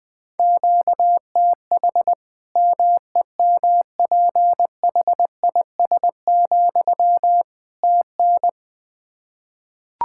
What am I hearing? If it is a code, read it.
Morse code, heard as a keyed tone, about 20 wpm, 700 Hz. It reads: QTH MEMPHIS, TN
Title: Practice QSO - Exchange 1 - Beginning - 20wpm